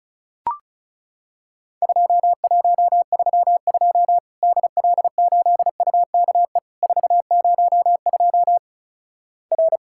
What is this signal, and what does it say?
Morse code, heard as a keyed tone, about 36 wpm, 700 Hz. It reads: 2132 DL8UKE 402 R
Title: Worked All Europe DX Contest - QTC Exchange - 35wpm